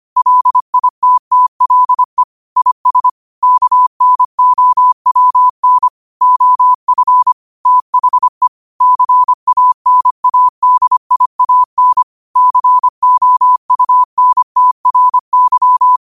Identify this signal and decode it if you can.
Morse code, heard as a keyed tone, about 25 wpm, 1000 Hz. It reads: LITTLE IS KNOWN OF THE CANADIAN COUNTRY
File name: qotdfiles / 25wpm